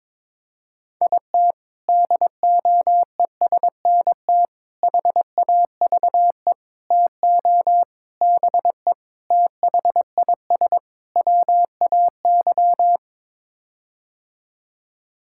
Morse code, heard as a keyed tone, about 22 wpm, 700 Hz. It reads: IT DOESNT HAVE TO BE THIS WAY
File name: practice / 22wpm / Sets of 7 Words - 22wpm